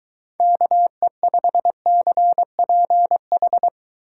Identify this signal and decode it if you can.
Morse code, heard as a keyed tone, about 23 wpm, 700 Hz. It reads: KE5CPH